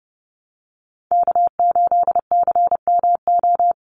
Morse code, heard as a keyed tone, about 30 wpm, 700 Hz. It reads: K8CMO
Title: Call Signs - 30wpm